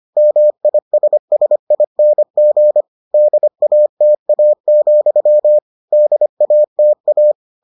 Morse code, heard as a keyed tone, about 25 wpm, 600 Hz. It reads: MISSING DATA, DATA